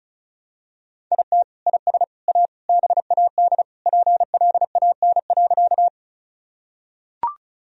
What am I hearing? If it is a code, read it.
Morse code, heard as a keyed tone, about 36 wpm, 700 Hz. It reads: IT IS A BAD PLAN.
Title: Sentences from Top 1000 Words - Review Part 2 - 35wpm